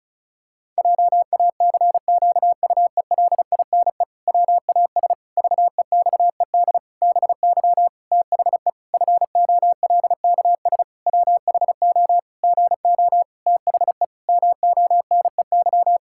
Morse code, heard as a keyed tone, about 34 wpm, 700 Hz. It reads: JACQUELINE WAS VEXED BY THE FOLKS WHO GO THE MONEY
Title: Pangram Challenge v2 - 35wpm